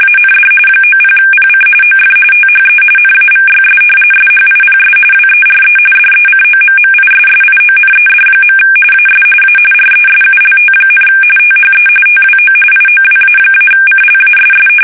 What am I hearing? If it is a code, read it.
RTTY, about 45.45 baud. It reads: AND 90PC OF THE NAVYS SHIPS WERE DECOMMISSIONED. AS A RESULT, THE TWO IOWA-CLASS
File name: fsk850 / battleshipnj